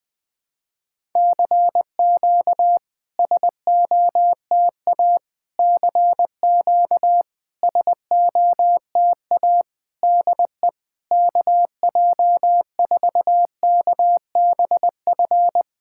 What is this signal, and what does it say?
Morse code, heard as a keyed tone, about 20 wpm, 700 Hz. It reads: CQ SOTA CQ SOTA DE KJ4KBF
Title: SOTA QSO Practice 20wpm